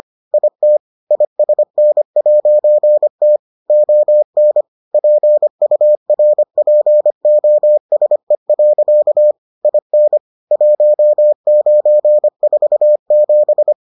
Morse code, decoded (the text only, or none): IT ISN'T ON PURPOSE. IN 1947